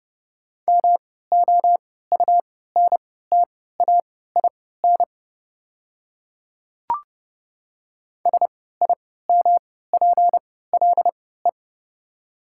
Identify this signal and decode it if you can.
Morse code, heard as a keyed tone, about 30 wpm, spaced as for 17 wpm, 690 Hz. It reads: MOUNTAIN SIMPLE
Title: 3x Character Spacing - Top 500 Words - 30wpm